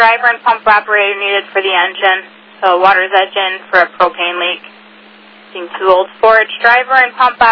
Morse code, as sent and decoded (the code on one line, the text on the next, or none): none